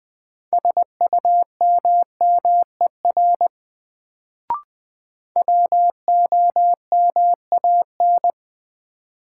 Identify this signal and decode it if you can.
Morse code, heard as a keyed tone, about 20 wpm, 700 Hz. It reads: SUMMER WOMAN
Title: Sentences from Top 1100 Words - New Word Drill Alt - 20wpm